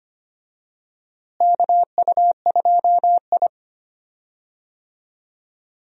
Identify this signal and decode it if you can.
Morse code, heard as a keyed tone, about 25 wpm, 700 Hz. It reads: KU2I
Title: USA Call Signs - 4 Characters - Repeated 2x - 25wpm